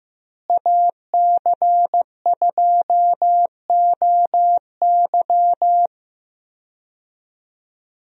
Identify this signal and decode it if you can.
Morse code, heard as a keyed tone, about 15 wpm, 700 Hz. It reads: AC2OY
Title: USA Call Signs - 5 Characters - Repeated 2x - 15wpm